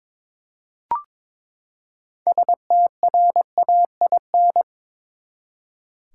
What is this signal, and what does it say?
Morse code, heard as a keyed tone, about 22 wpm, 700 Hz. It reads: STRAIN
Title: Six Letter Words - 22wpm